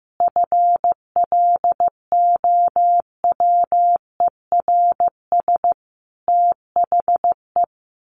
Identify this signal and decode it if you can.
Morse code, heard as a keyed tone, about 15 wpm, 700 Hz. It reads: FLOWERS THE
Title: A Sound Of Thunder 15wpm